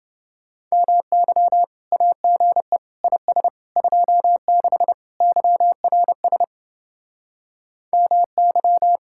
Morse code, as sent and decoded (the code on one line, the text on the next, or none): -- -.-- / .- --. / .. ... / ..--- -.... / -.-- .-. ... / -- -.--
MY AGE IS 26 YRS MY